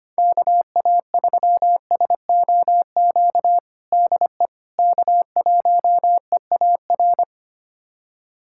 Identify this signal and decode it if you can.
Morse code, heard as a keyed tone, about 25 wpm, 700 Hz. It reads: KA3SOQ DE K1EAR
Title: USA Callsign DE Callsign 25wpm